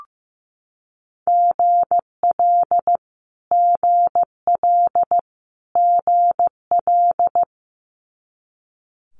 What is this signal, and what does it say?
Morse code, heard as a keyed tone, about 15 wpm, 700 Hz. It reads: GL GL GL
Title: Multi-letter Phonemes Repeated 3x 15wpm